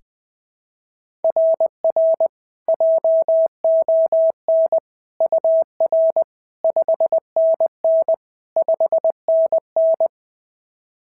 Morse code, heard as a keyed tone, about 20 wpm, 690 Hz. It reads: RR JON UR 5NN 5NN
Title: SOTA QSO Practice 20wpm